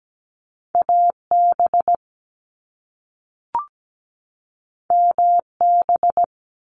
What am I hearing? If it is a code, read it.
Morse code, heard as a keyed tone, about 17 wpm, 700 Hz. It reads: AB MB
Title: Canadian Provinces and Territories 17wpm